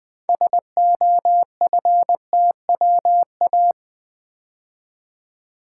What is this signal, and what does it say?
Morse code, heard as a keyed tone, about 20 wpm, 700 Hz. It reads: SOFTWA
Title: Build A Word - 3 Letters At A Time - 8-Letter Words - 20wpm